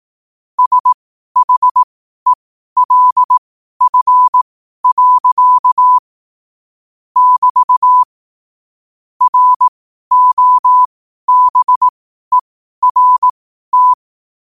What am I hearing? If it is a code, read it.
Morse code, heard as a keyed tone, about 18 wpm, spaced as for 12 wpm, 1000 Hz. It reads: SHELF. <BT> ROBERT
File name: qotdfiles / 12wpm